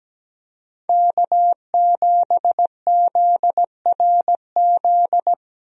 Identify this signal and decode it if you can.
Morse code, heard as a keyed tone, about 17 wpm, 700 Hz. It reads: K7ZRZ
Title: USA Call Signs - 5 Characters - Repeated 2x - 17wpm